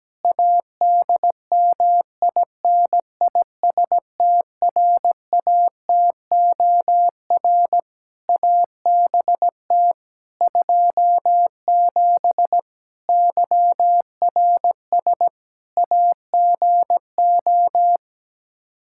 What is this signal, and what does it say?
Morse code, heard as a keyed tone, about 17 wpm, 700 Hz. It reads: ADMINISTRATOR ABT 27 YRS AGO
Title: K1USN SST QSO Practice 17wpm